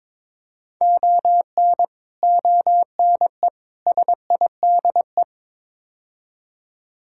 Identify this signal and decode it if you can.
Morse code, heard as a keyed tone, about 22 wpm, 700 Hz. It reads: ON ONE SIDE
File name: practice / 22wpm / Build A Phrase - 5-Word Phrases - 22wpm